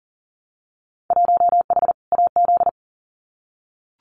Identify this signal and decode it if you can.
Morse code, heard as a keyed tone, about 38 wpm, 700 Hz. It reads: 1H AZ
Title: Winter Field Day Exchanges 40wpm